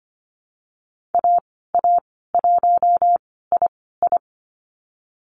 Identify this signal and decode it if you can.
Morse code, heard as a keyed tone, about 25 wpm, spaced as for 16 wpm, 700 Hz. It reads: AA1II
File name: Lesson 15g - Number 5 - Call Signs 25wpm